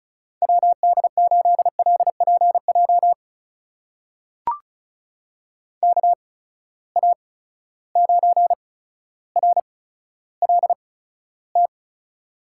Morse code, heard as a keyed tone, about 36 wpm, spaced as for 15 wpm, 700 Hz. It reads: WD8LPJ KA9RLT
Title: USA Call Signs - 6 Characters - 8x Character Spacing - 35wpm